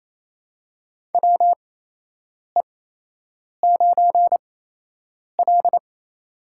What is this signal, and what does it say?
Morse code, heard as a keyed tone, about 28 wpm, spaced as for 8 wpm, 700 Hz. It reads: WE9L
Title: USA Call Signs - 4 Characters - 8x Character Spacing - 28wpm